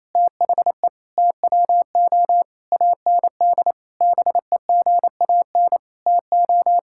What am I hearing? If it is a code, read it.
Morse code, heard as a keyed tone, about 28 wpm, 700 Hz. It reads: THE TWO AND BEGAN TO
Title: Sentences from Top 1000 Words - Review Part 5 - 28wpm